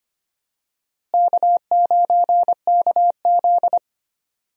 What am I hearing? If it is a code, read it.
Morse code, heard as a keyed tone, about 25 wpm, 700 Hz. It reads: K9KZ
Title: USA Call Signs - 4 Characters - Repeated 3x - 25wpm